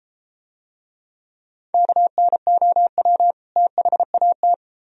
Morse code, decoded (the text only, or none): KNOW THAT